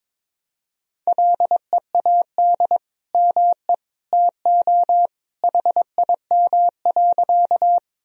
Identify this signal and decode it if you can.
Morse code, heard as a keyed tone, about 22 wpm, 700 Hz. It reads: LEAD ME TO HIM.
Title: Sentences from Top 1000 Words - Review Part 2 - 22wpm